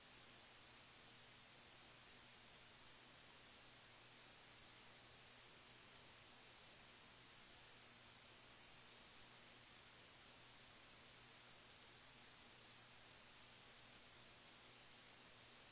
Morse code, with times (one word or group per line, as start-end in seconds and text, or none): none